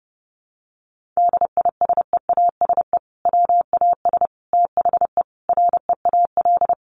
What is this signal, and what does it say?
Morse code, heard as a keyed tone, about 30 wpm, 700 Hz. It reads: DISEASE WAS THE REAL
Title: Sentences from Top 1100 Words 30wpm